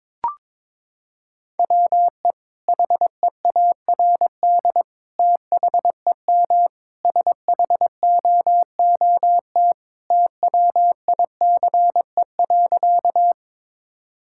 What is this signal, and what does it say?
Morse code, heard as a keyed tone, about 22 wpm, 700 Hz. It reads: WE HEARD THEM SHOOT TWICE.
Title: Sentences from Top 1600 Words 22wpm